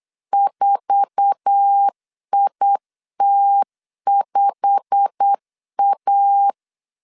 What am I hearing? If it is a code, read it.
Morse code, heard as a keyed tone, about 8 wpm, 800 Hz. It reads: 4IT5A
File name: CW / DET / 1 9 1500C S B 8wpm 5624